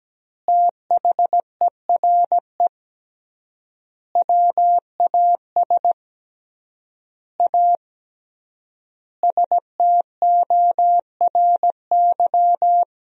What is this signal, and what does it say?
Morse code, heard as a keyed tone, about 17 wpm, 700 Hz. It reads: THERE WAS A STORY
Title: 3x Word-Spacing - Sentences from Top 500 Words - 17wpm